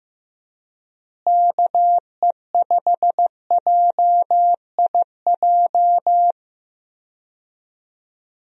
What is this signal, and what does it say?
Morse code, heard as a keyed tone, about 15 wpm, 700 Hz. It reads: KE5JIJ
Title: USA Call Signs - 6 Characters - Repeated 3x - 15wpm